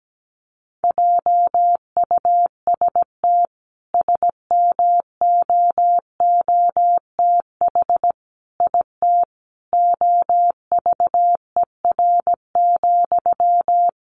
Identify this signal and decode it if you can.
Morse code, heard as a keyed tone, about 17 wpm, 700 Hz. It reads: JUST SMOOTH IT OVER,
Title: Sentences from Top 1400 Words 17wpm